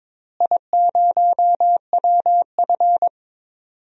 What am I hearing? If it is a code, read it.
Morse code, heard as a keyed tone, about 22 wpm, 700 Hz. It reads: I0WF
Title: DXCC-100 Call Signs - 4 Characters - Repeated 2x - 22wpm